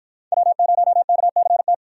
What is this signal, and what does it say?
Morse code, heard as a keyed tone, about 49 wpm, 700 Hz. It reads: W0KCT